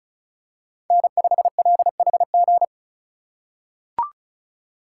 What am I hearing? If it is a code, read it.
Morse code, heard as a keyed tone, about 34 wpm, 700 Hz. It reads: N5LHG
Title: USA Call Signs - 3x Character Spacing - 35wpm